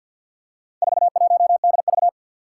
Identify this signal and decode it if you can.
Morse code, heard as a keyed tone, about 51 wpm, 700 Hz. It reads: 41DV
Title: Four Challenging Letter-Number Combinations 50wpm